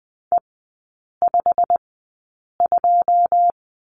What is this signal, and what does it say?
Morse code, heard as a keyed tone, about 20 wpm, spaced as for 8 wpm, 700 Hz. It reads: E52
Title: 121 Lesson 20e - Number 3 - Three Characters 20wpm